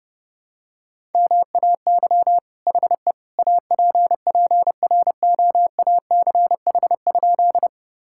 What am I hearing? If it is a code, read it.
Morse code, heard as a keyed tone, about 32 wpm, 700 Hz. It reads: MAY HE APPROACH?
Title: Sentences from Top 1000 Words - Review Part 6 - 30wpm